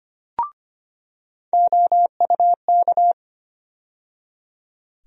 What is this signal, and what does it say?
Morse code, heard as a keyed tone, about 25 wpm, 700 Hz. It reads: OUK
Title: USA Call Sign Postfixes - 3 Letters 25wpm